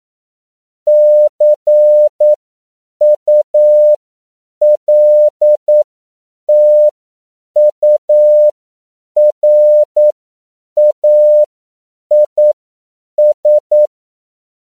Morse code, heard as a keyed tone, about 9 wpm, spaced as for 7 wpm, 600 Hz. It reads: CULTURAIS
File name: CW / 10PPM